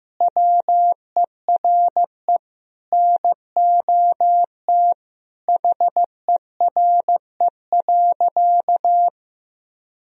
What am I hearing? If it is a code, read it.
Morse code, heard as a keyed tone, about 15 wpm, 700 Hz. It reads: WERE NOT HERE.
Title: Sentences from Top 1000 Words - Review Part 1 - 15wpm